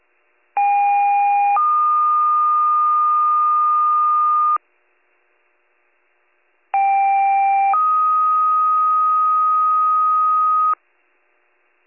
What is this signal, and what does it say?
Two-tone page, heard as a one-second tone, then a three-second tone, about 810 then 1220 Hz, about 790 then 1250 Hz. It is heard from 0.6 to 10.7 s.